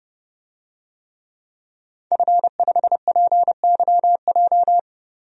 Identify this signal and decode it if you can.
Morse code, heard as a keyed tone, about 29 wpm, 700 Hz. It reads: F5PYJ
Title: DXCC-100 Call Signs - 5 Characters - Repeated 2x - 30wpm